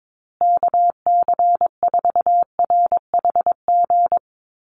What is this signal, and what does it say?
Morse code, heard as a keyed tone, about 22 wpm, 700 Hz. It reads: KC4RHG
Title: USA Call Signs Repeated 2x 22wpm